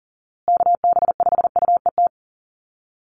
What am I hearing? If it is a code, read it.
Morse code, heard as a keyed tone, about 40 wpm, 700 Hz. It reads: KB5UET
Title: USA Call Signs Repeated 3x 40wpm